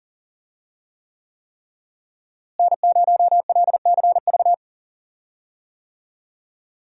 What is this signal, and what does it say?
Morse code, heard as a keyed tone, about 40 wpm, 700 Hz. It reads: N0LCV
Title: USA Call Signs - 5 Characters - Repeated 3x - 40wpm